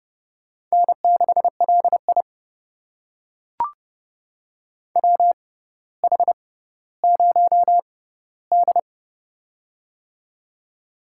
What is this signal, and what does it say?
Morse code, heard as a keyed tone, about 30 wpm, spaced as for 15 wpm, 700 Hz. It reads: N6LI WH0D
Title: USA Call Signs - 6x Character Spacing - 30wpm